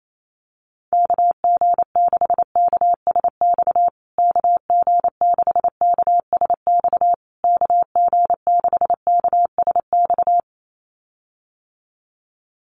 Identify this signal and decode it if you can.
Morse code, heard as a keyed tone, about 28 wpm, 700 Hz. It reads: KG6KSX KG6KSX KG6KSX
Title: USA Call Signs Repeated 3x 28wpm